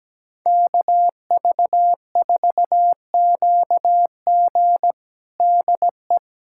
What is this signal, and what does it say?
Morse code, heard as a keyed tone, about 17 wpm, 700 Hz. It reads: KV4QG DE
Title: USA Callsign DE Callsign - 5 Characters - 17wpm